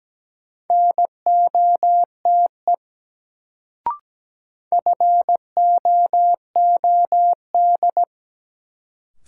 Morse code, heard as a keyed tone, about 17 wpm, 700 Hz. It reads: NOTE FOOD
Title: Sets of 5 Words - New Word Drill - 17wpm